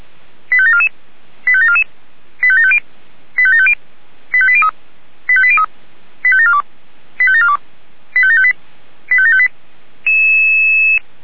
Five-tone selective call (ZVEI), heard as quick runs of five tones, tones of about 70 ms. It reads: 85744 85744 85750 85750 85803 85803 85742 85742 85758 85758